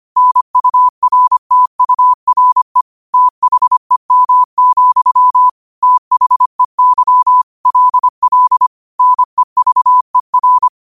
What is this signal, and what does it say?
Morse code, heard as a keyed tone, about 25 wpm, 1000 Hz. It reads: NURTURE THEM, THEY LL NEVER
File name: qotdfiles / 25wpm